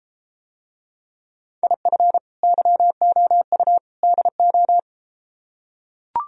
Morse code, heard as a keyed tone, about 34 wpm, 700 Hz. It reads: IF YOU DO